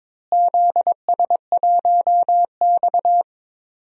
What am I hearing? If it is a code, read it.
Morse code, heard as a keyed tone, about 22 wpm, 700 Hz. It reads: ZS1X
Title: DXCC-100 Call Signs - 4 Characters - Repeated 2x - 22wpm